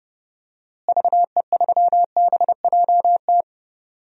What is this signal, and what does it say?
Morse code, heard as a keyed tone, about 29 wpm, 700 Hz. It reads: VE3BJT
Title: DXCC-100 Call Signs - 6 Characters - Repeated 3x - 30wpm